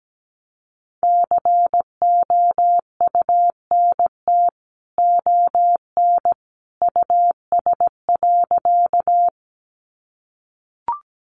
Morse code, heard as a keyed tone, about 17 wpm, 700 Hz. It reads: COUNT ON US.
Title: Sentences from Top 1500 Words 17wpm